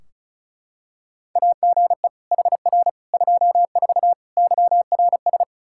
Morse code, heard as a keyed tone, about 36 wpm, 700 Hz. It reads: AGE HR 24 YRS
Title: K1USN SST QSO Practice 35wpm